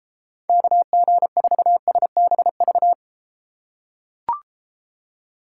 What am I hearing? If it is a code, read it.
Morse code, heard as a keyed tone, about 33 wpm, 700 Hz. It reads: KG4SBV